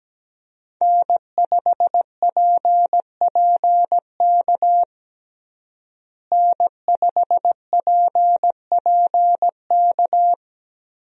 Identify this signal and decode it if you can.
Morse code, heard as a keyed tone, about 17 wpm, 700 Hz. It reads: N5PPK N5PPK
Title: USA Call Signs - 5 Characters - Repeated 3x - 17wpm